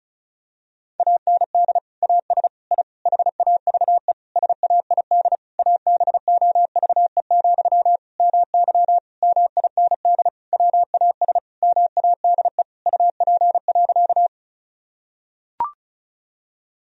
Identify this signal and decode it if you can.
Morse code, heard as a keyed tone, about 34 wpm, 700 Hz. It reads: AND AS I HAVE SAID ABOVE, MY MIND WAS MADE UP.
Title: Sentences from Top 1000 Words - Review Part 2 - 35wpm